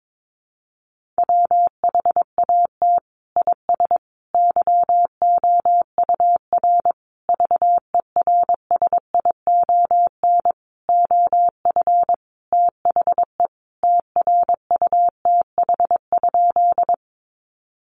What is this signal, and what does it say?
Morse code, heard as a keyed tone, about 22 wpm, 700 Hz. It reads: WHAT IS YOUR VERSION OF THE TRUTH?
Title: Sentences from Top 1400 Words 22wpm